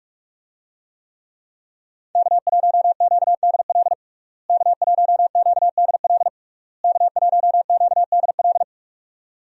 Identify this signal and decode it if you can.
Morse code, heard as a keyed tone, about 45 wpm, 700 Hz. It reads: K1QDL K1QDL K1QDL
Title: USA Call Signs - 5 Characters - Repeated 3x - 45wpm